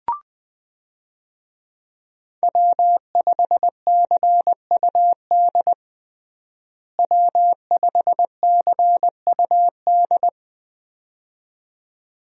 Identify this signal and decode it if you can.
Morse code, heard as a keyed tone, about 20 wpm, 700 Hz. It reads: W5CUD W5CUD
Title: USA Call Signs - 5 Characters - Repeated 2x - 20wpm